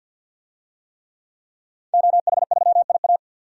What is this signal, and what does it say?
Morse code, heard as a keyed tone, about 52 wpm, 700 Hz. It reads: OH3IA
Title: DXCC-100 Call Signs - 5 Characters - Repeated 3x - 50wpm